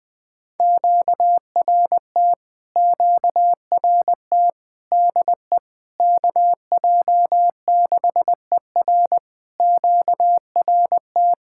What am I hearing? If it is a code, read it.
Morse code, heard as a keyed tone, about 20 wpm, 700 Hz. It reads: QRT QRT DE KJ6ER QRT
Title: POTA QSO Practice - 20wpm